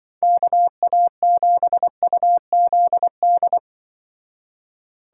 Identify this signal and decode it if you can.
Morse code, heard as a keyed tone, about 24 wpm, 700 Hz. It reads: KA7UZD
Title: Speed Racing - USA 2x3 Call Signs 20wpm